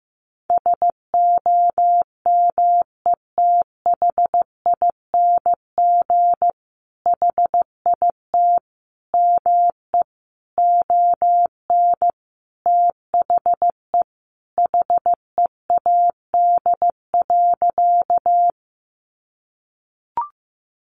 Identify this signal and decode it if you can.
Morse code, heard as a keyed tone, about 15 wpm, 700 Hz. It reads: SOMETHING HIT ME ON THE HEAD.
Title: Sentences from Top 1200 Words 15wpm